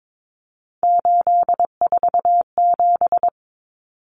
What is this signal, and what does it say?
Morse code, heard as a keyed tone, about 22 wpm, 700 Hz. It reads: 847
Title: Numbers - 3-Digit 22wpm